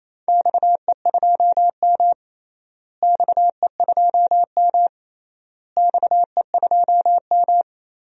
Morse code, heard as a keyed tone, about 28 wpm, 700 Hz. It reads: XE2M XE2M XE2M
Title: DXCC-100 Call Signs - 4 Characters - Repeated 3x - 28wpm